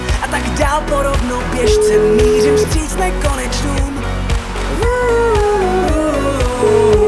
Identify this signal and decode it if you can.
Ringback tone (Europe, 425 Hz), with speech.